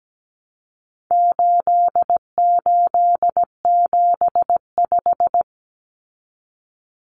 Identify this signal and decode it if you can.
Morse code, heard as a keyed tone, about 17 wpm, 700 Hz. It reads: 8875
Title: Common Cut Numbers - 4-Digit 17wpm